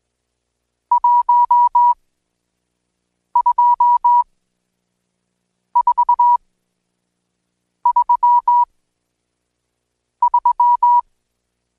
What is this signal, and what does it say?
Morse code, heard as a keyed tone, about 20 wpm, spaced as for 5 wpm, 970 Hz. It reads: 12433
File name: Code Quick / AudioDownld1 / Seg76-Day8-DayDrill 8